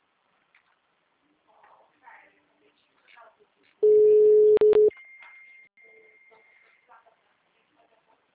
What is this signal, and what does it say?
Ringback tone (Europe, 425 Hz), with speech.
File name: out- -2003- - - .178